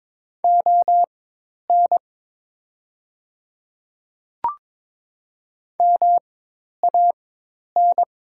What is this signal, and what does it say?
Morse code, heard as a keyed tone, about 22 wpm, spaced as for 10 wpm, 700 Hz. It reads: ON MAN
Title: 4x Character Spacing - Top 200 Words - 22wpm